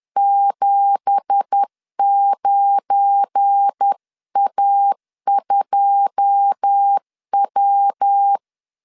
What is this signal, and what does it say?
Morse code, heard as a keyed tone, about 10 wpm, 800 Hz. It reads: 79A2W
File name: CW / DET / 1 9 1500C S B 10wpm 4439